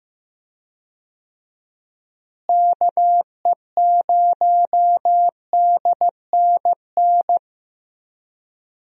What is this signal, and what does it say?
Morse code, heard as a keyed tone, about 15 wpm, 700 Hz. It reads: KE0DNN
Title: USA Call Signs - 6 Characters - Repeated 2x - 15wpm